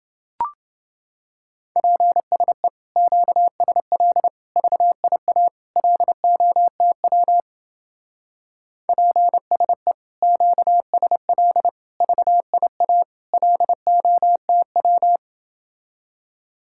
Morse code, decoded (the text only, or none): PSE QSL VIA LOTW PSE QSL VIA LOTW